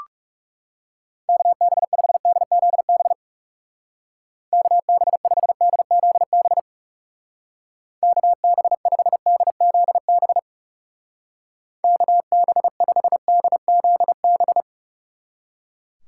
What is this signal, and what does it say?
Morse code, heard as a keyed tone, about 38 wpm, 700 Hz. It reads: KB5DZB KB5DZB KB5DZB KB5DZB